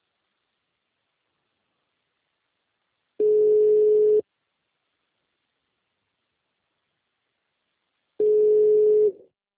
Ringback tone (Europe, 425 Hz).